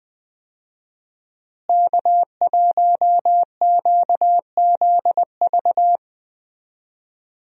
Morse code, heard as a keyed tone, about 20 wpm, 700 Hz. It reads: K1QZV
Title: USA Call Signs - 5 Characters - Repeated 2x - 20wpm